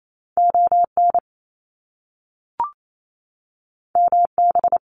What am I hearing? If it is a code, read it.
Morse code, heard as a keyed tone, about 28 wpm, 700 Hz. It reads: ON MB
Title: Canadian Provinces and Territories 28wpm